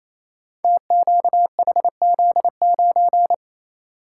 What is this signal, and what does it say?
Morse code, heard as a keyed tone, about 28 wpm, 700 Hz. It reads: TQHZ9
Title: Code Groups - 5 Letter:Numbers - 28wpm